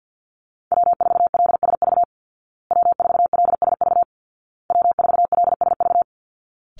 Morse code, heard as a keyed tone, about 50 wpm, 700 Hz. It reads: W4LSV W4LSV W4LSV
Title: Call Signs Repeated 2x - 50wpm